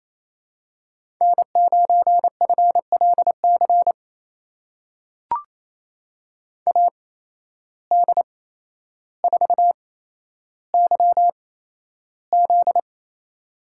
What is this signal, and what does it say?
Morse code, heard as a keyed tone, about 28 wpm, spaced as for 12 wpm, 700 Hz. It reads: N9FLC AD4YZ
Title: USA Call Signs - 5 Characters - 8x Character Spacing - 28wpm